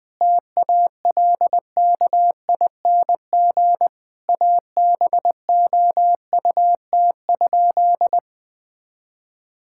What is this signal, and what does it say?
Morse code, heard as a keyed tone, about 20 wpm, 700 Hz. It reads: TALKING ABOUT?
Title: Sentences with 2-Word Phrases - 20wpm